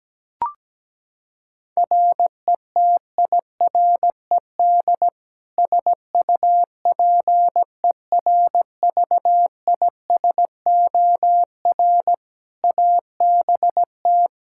Morse code, heard as a keyed tone, about 17 wpm, 700 Hz. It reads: RETIRED SUPERVISOR ABT
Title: Practice QSO - Exchange 3 - More Info - 17wpm